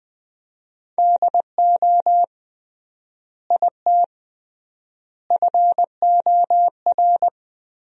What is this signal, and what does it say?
Morse code, heard as a keyed tone, about 21 wpm, 700 Hz. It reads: DO IT FOR
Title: 3x Word-Spacing - Sentences from Top 200 Words - 20wpm